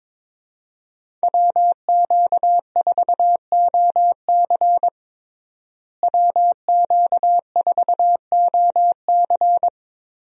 Morse code, heard as a keyed tone, about 22 wpm, 700 Hz. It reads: WQ4OC WQ4OC
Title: USA Call Signs - 5 Characters - Repeated 3x - 22wpm